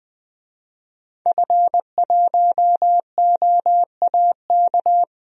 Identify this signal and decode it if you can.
Morse code, heard as a keyed tone, about 20 wpm, 700 Hz. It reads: F1OAK
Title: DXCC-100 Call Signs - 5 Characters - Repeated 3x - 20wpm